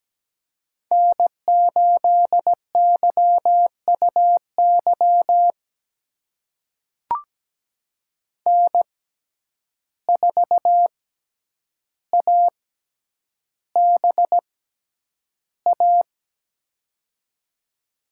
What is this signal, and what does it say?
Morse code, heard as a keyed tone, about 17 wpm, spaced as for 9 wpm, 700 Hz. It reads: N8YUY N4ABA
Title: USA Call Signs - 5 Characters - 6x Character Spacing - 17wpm